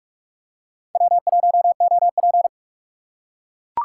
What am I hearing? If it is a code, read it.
Morse code, heard as a keyed tone, about 43 wpm, 700 Hz. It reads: W1OP